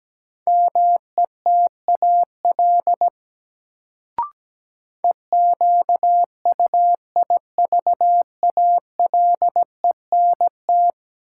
Morse code, heard as a keyed tone, about 17 wpm, 700 Hz. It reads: METAL EQUIVALENT
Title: Sentences from Top 1600 Words - New Word Drill Alt - 17wpm